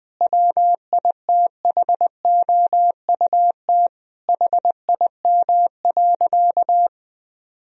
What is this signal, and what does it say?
Morse code, heard as a keyed tone, about 20 wpm, 700 Hz. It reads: WITHOUT HIM.
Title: Sentences from Top 1000 Words - Review Part 3 - 20wpm